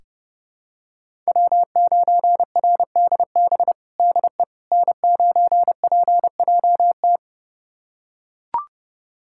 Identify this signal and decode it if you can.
Morse code, heard as a keyed tone, about 30 wpm, 700 Hz. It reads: W9RDB DE N9PJT
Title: USA Callsign DE Callsign - 5 Characters - 30wpm